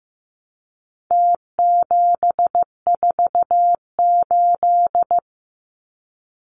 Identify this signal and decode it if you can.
Morse code, heard as a keyed tone, about 15 wpm, 700 Hz. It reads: T748
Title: Numbers - Contest Style - 4-Digit 15wpm